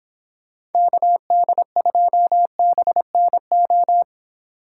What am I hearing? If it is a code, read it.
Morse code, heard as a keyed tone, about 28 wpm, 700 Hz. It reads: KD2BNO